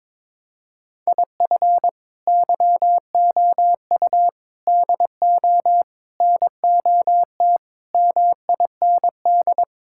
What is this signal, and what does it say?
Morse code, heard as a keyed tone, about 22 wpm, 700 Hz. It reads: IF YOU DO NOT MIND
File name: practice / 22wpm / Sentences with 4-Word Phrases - 22wpm